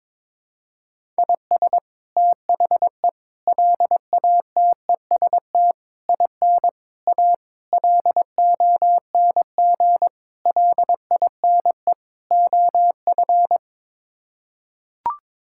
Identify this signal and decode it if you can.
Morse code, heard as a keyed tone, about 22 wpm, 700 Hz. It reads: IS THE LATEST IN A LONG LINE OF
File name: practice / 22wpm / Sets of 8 Words - 22wpm